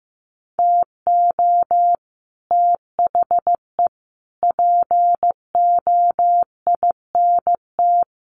Morse code, heard as a keyed tone, about 15 wpm, 700 Hz. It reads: TO THE POINT
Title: Sets of 4 Words 15wpm